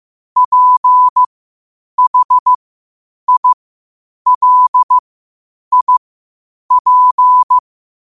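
Morse code, heard as a keyed tone, about 15 wpm, spaced as for 8 wpm, 1000 Hz. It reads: PHILIP
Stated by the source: Morse code quotes at 8 words per minute from QOTD at 8 WPM.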